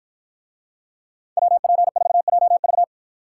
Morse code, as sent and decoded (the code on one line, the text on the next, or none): .-- .--. ....- .--- ...-
WP4JV